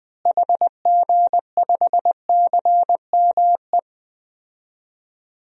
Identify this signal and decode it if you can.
Morse code, heard as a keyed tone, about 20 wpm, 700 Hz. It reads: HG5CME
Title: DXCC-100 Call Signs - 6 Characters - Repeated 3x - 20wpm